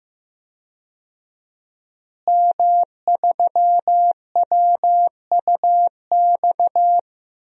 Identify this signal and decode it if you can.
Morse code, heard as a keyed tone, about 15 wpm, 700 Hz. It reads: M3WUX
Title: DXCC-100 Call Signs - 5 Characters - Repeated 3x - 15wpm